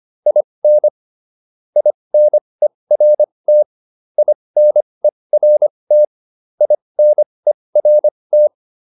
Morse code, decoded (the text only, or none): IN INERT INERT INERT